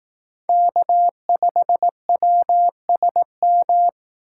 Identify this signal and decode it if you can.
Morse code, heard as a keyed tone, about 18 wpm, 700 Hz. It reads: K5WSM
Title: Speed Racing - USA 1x3 Call Signs 15wpm